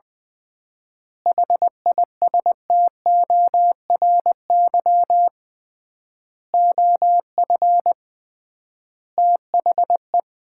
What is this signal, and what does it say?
Morse code, heard as a keyed tone, about 20 wpm, 700 Hz. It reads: HISTORY OF THE
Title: 3x Word-Spacing - Sets of 5 Words - 20wpm